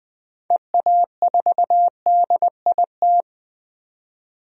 Morse code, heard as a keyed tone, about 20 wpm, 700 Hz. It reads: EA4DIT
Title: DXCC-100 Call Signs - 6 Characters - 20wpm